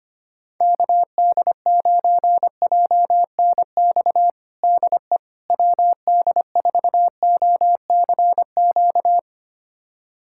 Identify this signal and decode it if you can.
Morse code, heard as a keyed tone, about 25 wpm, 700 Hz. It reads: KD9JNX DE WD4OCQ
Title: USA Callsign DE Callsign 25wpm